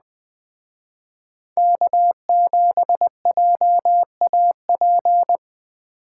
Morse code, heard as a keyed tone, about 20 wpm, 700 Hz. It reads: K7JAP
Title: Speed Racing - USA 1x3 Call Signs 17wpm